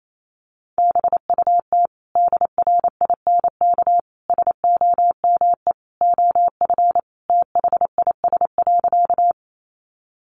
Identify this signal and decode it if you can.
Morse code, heard as a keyed tone, about 28 wpm, 700 Hz. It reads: BUT DRINK SOME OF THIS.
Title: Sentences from Top 1500 Words 28wpm